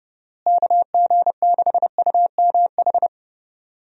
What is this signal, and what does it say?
Morse code, heard as a keyed tone, about 30 wpm, 700 Hz. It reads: KG6UMH
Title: USA Call Signs - 6 Characters - Repeated 2x - 30wpm